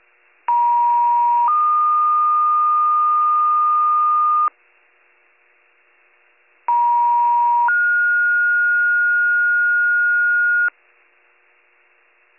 Two-tone page, heard as a one-second tone, then a three-second tone, about 950 then 1220 Hz, about 950 then 1430 Hz. It is heard from 0.5 to 10.7 s.